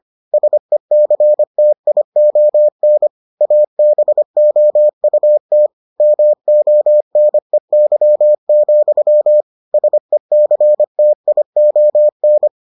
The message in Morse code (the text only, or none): SECTION ABOUT MONEY, SECTION